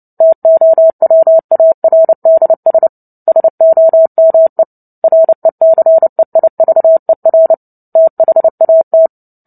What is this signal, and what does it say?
Morse code, heard as a keyed tone, about 30 wpm, 650 Hz. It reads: TOWARDS SOME RECEIVER THAT
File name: Book Two / 03-Chapter 3 - 29 WPM